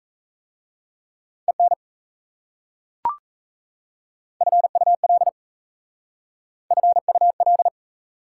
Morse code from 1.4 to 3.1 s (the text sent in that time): EN